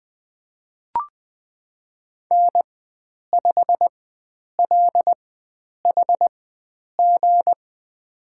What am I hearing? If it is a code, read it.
Morse code, heard as a keyed tone, about 20 wpm, spaced as for 9 wpm, 700 Hz. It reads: N5LHG
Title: USA Call Signs - 4x Character Spacing - 20wpm